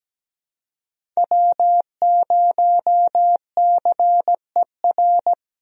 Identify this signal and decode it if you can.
Morse code, heard as a keyed tone, about 17 wpm, 700 Hz. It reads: W0CER
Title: USA Call Signs - 5 Characters - Repeated 3x - 17wpm